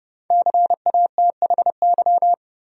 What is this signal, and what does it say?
Morse code, heard as a keyed tone, about 30 wpm, 700 Hz. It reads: CATHY
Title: USA Common Names - Max 5 Letters - 30wpm